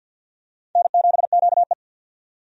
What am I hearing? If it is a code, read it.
Morse code, heard as a keyed tone, about 48 wpm, 700 Hz. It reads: N7QE
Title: USA Call Signs - 4 Characters - 50wpm